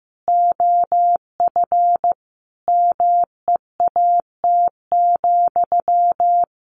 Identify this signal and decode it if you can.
Morse code, heard as a keyed tone, about 15 wpm, 700 Hz. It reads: OF MEAT,
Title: A Sound Of Thunder 15wpm